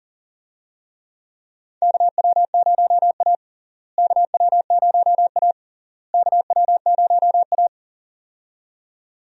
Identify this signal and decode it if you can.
Morse code, heard as a keyed tone, about 40 wpm, 700 Hz. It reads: KW0A KW0A KW0A
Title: USA Call Signs - 4 Characters - Repeated 3x - 40wpm